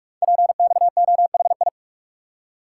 Morse code, heard as a keyed tone, about 44 wpm, 690 Hz. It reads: PXOHI
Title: Code Groups - 5 Letters - 45wpm